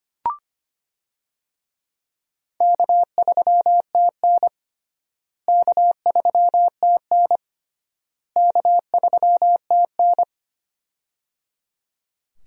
Morse code, heard as a keyed tone, about 25 wpm, 700 Hz. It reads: K3TN K3TN K3TN
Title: USA Call Signs - 4 Characters - Repeated 3x - 25wpm